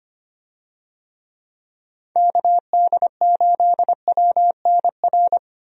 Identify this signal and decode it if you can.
Morse code, heard as a keyed tone, about 25 wpm, 700 Hz. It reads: KD8WNR
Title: USA Call Signs - 6 Characters - 25wpm